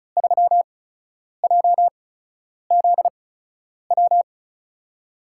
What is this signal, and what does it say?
Morse code, heard as a keyed tone, about 36 wpm, 700 Hz. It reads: 3 J Z W
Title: USA Call Signs - 6 Characters - 8x Character Spacing - 35wpm